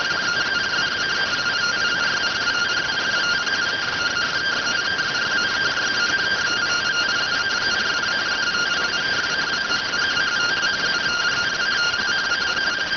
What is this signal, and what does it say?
Two-tone teleprinter signal, about 170 Hz shift.